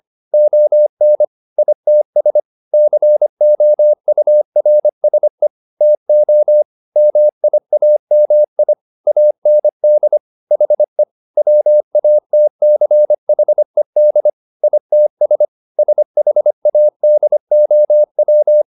Morse code, heard as a keyed tone, about 25 wpm, 600 Hz. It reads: ON ITS COURSE TO MIAMI AND HE WATCHED ITS SHADOW